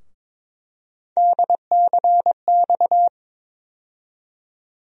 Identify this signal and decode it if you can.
Morse code, heard as a keyed tone, about 22 wpm, 700 Hz. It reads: DCX
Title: USA Call Sign Postfixes - 3 Letters 22wpm